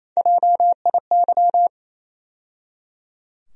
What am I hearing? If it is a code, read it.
Morse code, heard as a keyed tone, about 27 wpm, 700 Hz. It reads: JIY
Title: USA Call Sign Postfixes - 3 Letters 28wpm